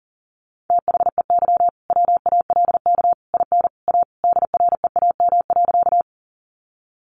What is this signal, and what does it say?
Morse code, heard as a keyed tone, about 40 wpm, 700 Hz. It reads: THEY WALK IN A DREAM.
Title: Sentences from Top 1500 Words 40wpm